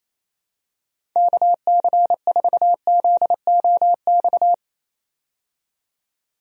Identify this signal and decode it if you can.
Morse code, heard as a keyed tone, about 28 wpm, 700 Hz. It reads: KC4ZOX
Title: USA Call Signs - 6 Characters - 28wpm